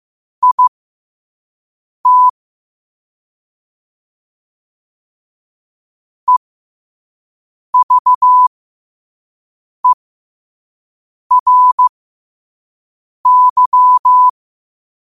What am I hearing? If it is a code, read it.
Morse code, heard as a keyed tone, about 15 wpm, spaced as for 5 wpm, 1000 Hz. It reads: IT EVERY
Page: New quotes every day in morse code at 5 Words per minute.